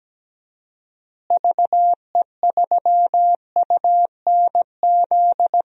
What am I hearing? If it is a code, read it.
Morse code, heard as a keyed tone, about 17 wpm, 700 Hz. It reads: VE3UNZ
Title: DXCC-100 Call Signs - 6 Characters - Repeated 2x - 17wpm